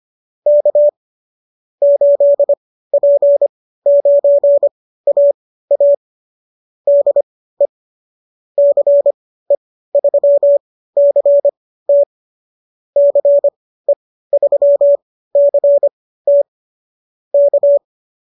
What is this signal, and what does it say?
Morse code, heard as a keyed tone, about 25 wpm, spaced as for 15 wpm, 590 Hz. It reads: K 8P9AA DE CE3CT CE3CT K